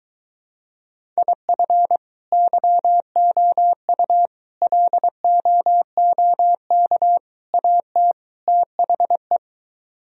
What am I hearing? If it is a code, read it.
Morse code, heard as a keyed tone, about 23 wpm, 700 Hz. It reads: IF YOU LOOK AT THE